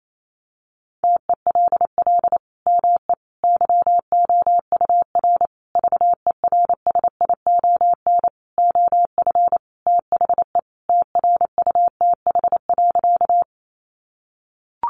Morse code, heard as a keyed tone, about 28 wpm, 700 Hz. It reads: TELL ME YOUR VERSION OF THE TRUTH.
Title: Sentences from Top 1400 Words 28wpm